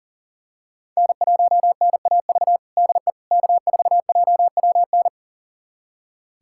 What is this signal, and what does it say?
Morse code, heard as a keyed tone, about 40 wpm, 700 Hz. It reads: N1NAV DE K4JWN
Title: USA Callsign DE Callsign - 5 Characters - 40wpm